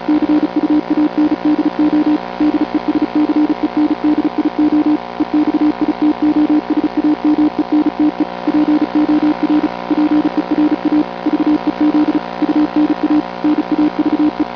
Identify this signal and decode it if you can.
Morse code, heard as a keyed tone, about 36 wpm, 300 Hz. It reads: CUANDO DESCENDIO EXITOSAMENTE POR PERA VEZ UNA NAVE